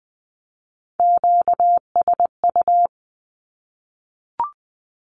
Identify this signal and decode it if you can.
Morse code, heard as a keyed tone, about 20 wpm, 700 Hz. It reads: QSU
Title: Q-Codes 20wpm